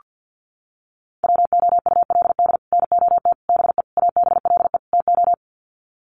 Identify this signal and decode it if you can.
Morse code, heard as a keyed tone, about 48 wpm, 700 Hz. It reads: WOULD NOT BE ABLE TO
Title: Sets of 5 Words 50wpm